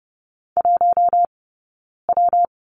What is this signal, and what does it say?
Morse code, heard as a keyed tone, about 30 wpm, 700 Hz. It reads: 1 W
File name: Lesson 19d - Letter w - Two Characters 30wpm